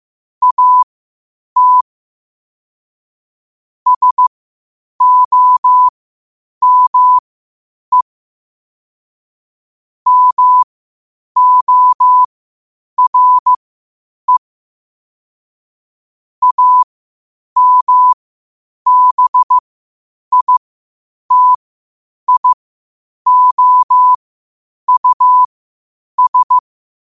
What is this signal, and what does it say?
Morse code, heard as a keyed tone, about 15 wpm, spaced as for 8 wpm, 1000 Hz. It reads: AT SOME MORE AMBITIOUS
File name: qotdfiles / 8wpm